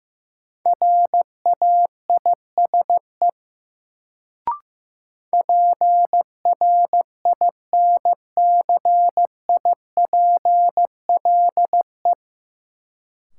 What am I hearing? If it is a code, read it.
Morse code, heard as a keyed tone, about 15 wpm, 700 Hz. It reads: RAISE PRINCIPLE
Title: Sentences from Top 1400 Words - New Word Drill - 15wpm